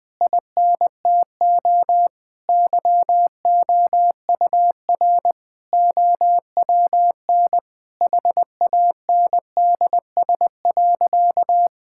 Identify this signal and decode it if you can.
Morse code, heard as a keyed tone, about 20 wpm, 700 Hz. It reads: INTO YOUR OWN HANDS.
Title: Sentences with 7-Word Phrases - 20wpm